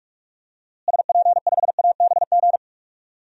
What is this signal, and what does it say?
Morse code, heard as a keyed tone, about 46 wpm, 700 Hz. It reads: SP5ABG
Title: DXCC-100 Call Signs - 6 Characters - Repeated 2x - 45wpm